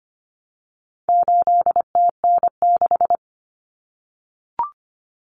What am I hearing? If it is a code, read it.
Morse code, heard as a keyed tone, about 25 wpm, 700 Hz. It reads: 8TN6
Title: Common Cut Numbers - 4-Digit 25wpm